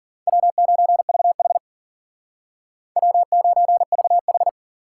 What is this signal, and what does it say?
Morse code, heard as a keyed tone, about 54 wpm, 700 Hz. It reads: W9VH W9VH